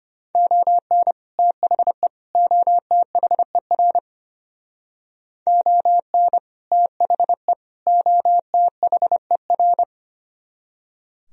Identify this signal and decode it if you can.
Morse code, heard as a keyed tone, about 30 wpm, 700 Hz. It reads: ON THE OTHER ON THE OTHER
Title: Speed-Racing - Sets of 3 Words 25wpm